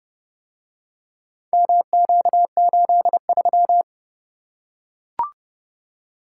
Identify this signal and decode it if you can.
Morse code, heard as a keyed tone, about 28 wpm, 700 Hz. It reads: MQ83